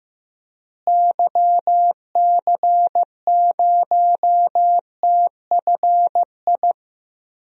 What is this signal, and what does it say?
Morse code, heard as a keyed tone, about 15 wpm, 700 Hz. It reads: YC0TFI
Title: DXCC-100 Call Signs - 6 Characters - Repeated 3x - 15wpm